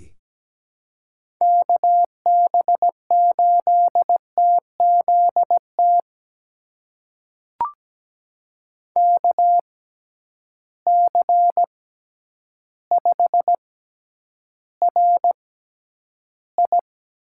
Morse code, heard as a keyed tone, about 17 wpm, spaced as for 9 wpm, 700 Hz. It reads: KB8TZT KC5RI
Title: USA Call Signs - 6 Characters - 6x Character Spacing - 17wpm